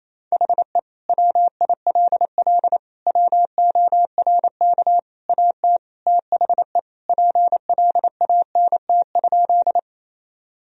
Morse code, heard as a keyed tone, about 28 wpm, 700 Hz. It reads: HE WILL WORK AT THE PLANT?
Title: Sentences from Top 1000 Words - Review Part 1 - 28wpm